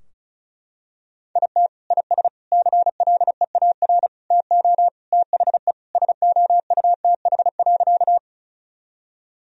Morse code, heard as a keyed tone, about 34 wpm, 700 Hz. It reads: IT IS CLEAR TO THE SOUTH.
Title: Sentences from Top 1000 Words - Review Part 2 - 35wpm